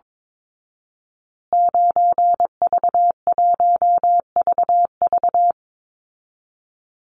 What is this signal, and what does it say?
Morse code, heard as a keyed tone, about 22 wpm, 700 Hz. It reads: 9V1VV
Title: Call Signs Repeated 2x - 22wpm